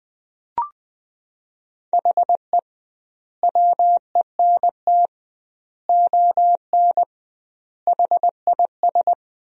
Morse code, heard as a keyed tone, about 20 wpm, 700 Hz. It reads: HE WENT ON HIS
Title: 2x Word-Spacing - Sentences from Top 200 Words - 20wpm